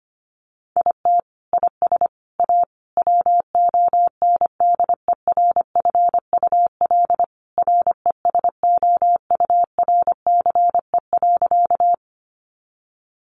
Morse code, heard as a keyed tone, about 25 wpm, 700 Hz. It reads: IT IS A WONDERFUL RESOURCE.
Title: Sentences from Top 1300 Words 25wpm